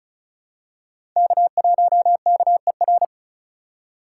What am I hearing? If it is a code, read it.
Morse code, heard as a keyed tone, about 36 wpm, 700 Hz. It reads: K1KER
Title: USA Call Signs - 5 Characters - 4x Character Spacing - 35wpm